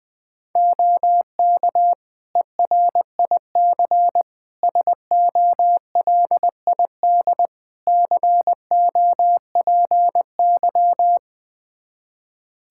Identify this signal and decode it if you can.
Morse code, heard as a keyed tone, about 20 wpm, 700 Hz. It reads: OK ERIC SOLID COPY
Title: Practice QSO - Exchange 3 - More Info - 20wpm